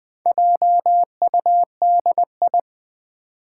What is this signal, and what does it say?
Morse code, heard as a keyed tone, about 20 wpm, 700 Hz. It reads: JUDI
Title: Build A Word - 2 Letters At A Time - 8-Letter Words - 20wpm